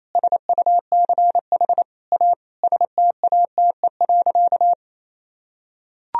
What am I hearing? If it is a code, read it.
Morse code, heard as a keyed tone, about 28 wpm, 700 Hz. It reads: SUCH A STATE.
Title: Sentences from Top 1000 Words - Review Part 2 - 28wpm